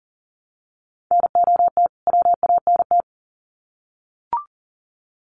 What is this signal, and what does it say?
Morse code, heard as a keyed tone, about 40 wpm, 700 Hz. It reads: NOT WANT
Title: Sets of 2 Words 40wpm